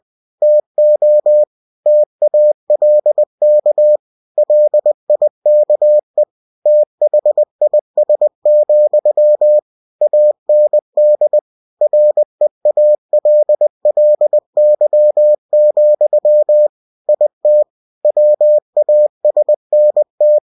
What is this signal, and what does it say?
Morse code, heard as a keyed tone, about 20 wpm, 600 Hz. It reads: TO TALK LIKE THIS, AND REALLY, IT WASNT
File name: cw-books / winnie-the-pooh / winnie-20wpm-page-0091